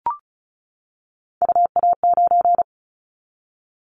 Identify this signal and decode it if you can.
Morse code, heard as a keyed tone, about 36 wpm, 700 Hz. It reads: UA9
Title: Callsign Prefixes for Top 25 DXCC - 35wpm